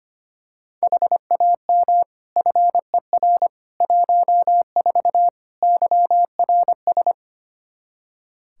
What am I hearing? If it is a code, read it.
Morse code, heard as a keyed tone, about 25 wpm, 700 Hz. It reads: HAM FER 14 YRS
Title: K1USN SST QSO Practice 25wpm